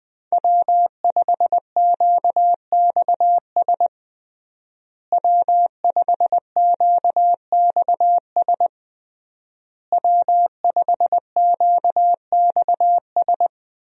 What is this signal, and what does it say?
Morse code, heard as a keyed tone, about 20 wpm, 700 Hz. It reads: W5QXS W5QXS W5QXS
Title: USA Call Signs - 5 Characters - Repeated 3x - 20wpm